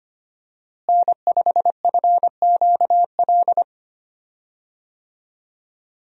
Morse code, heard as a keyed tone, about 25 wpm, 700 Hz. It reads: N5FQL
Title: USA Call Signs - 5 Characters - Repeated 2x - 25wpm